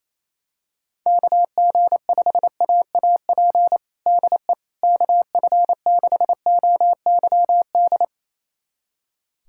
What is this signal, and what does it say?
Morse code, heard as a keyed tone, about 28 wpm, 700 Hz. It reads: KG5AAP DE KF6OYD
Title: USA Callsign DE Callsign - 6 Characters - 28wpm